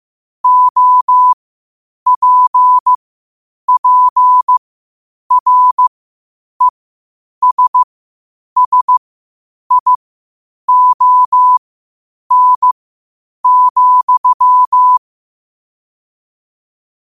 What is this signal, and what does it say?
Morse code, heard as a keyed tone, about 15 wpm, spaced as for 8 wpm, 1000 Hz. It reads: OPPRESSION,
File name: qotdfiles / 8wpm